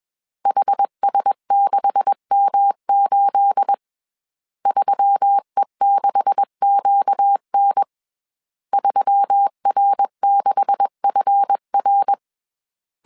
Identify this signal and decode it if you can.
Morse code, heard as a keyed tone, about 21 wpm, 800 Hz. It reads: HS6M8 3E6QN 3R6FR
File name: CW / DET / 1 9 1500C S C 20wpm 2310